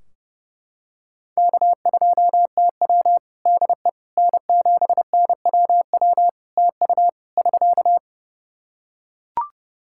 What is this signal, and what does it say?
Morse code, heard as a keyed tone, about 30 wpm, 700 Hz. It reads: K2TW DE N7NWW TU <SK>
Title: K1USN SST QSO Practice 30wpm